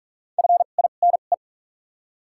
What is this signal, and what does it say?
Morse code, heard as a keyed tone, about 47 wpm, spaced as for 33 wpm, 700 Hz. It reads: FINE
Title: 2x Character Spacing - Top 500 Words - 45wpm